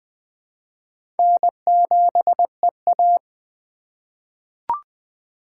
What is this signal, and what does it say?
Morse code, heard as a keyed tone, about 20 wpm, 700 Hz. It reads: N7EA
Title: USA Call Signs - 3x Character Spacing - 20wpm